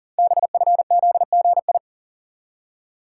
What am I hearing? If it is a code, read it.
Morse code, heard as a keyed tone, about 40 wpm, 700 Hz. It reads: BFZGI
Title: Code Groups - 5 Letters - 40wpm